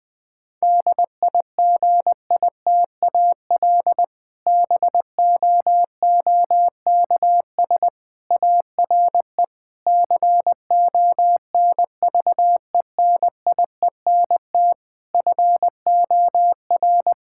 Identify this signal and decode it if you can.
Morse code, heard as a keyed tone, about 20 wpm, 700 Hz. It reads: DIGITAL BOOKS ARE CONVENIENT FOR
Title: Difficult Sentences with Hints - 20wpm